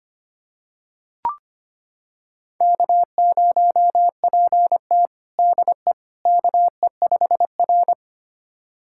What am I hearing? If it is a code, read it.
Morse code, heard as a keyed tone, about 25 wpm, 700 Hz. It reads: K0PT DE KE5R
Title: USA Callsign DE Callsign - 4 Characters - 25wpm